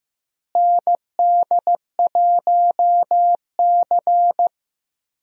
Morse code, heard as a keyed tone, about 15 wpm, 700 Hz. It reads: ND1C
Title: USA Call Signs - 4 Characters - Repeated 3x - 15wpm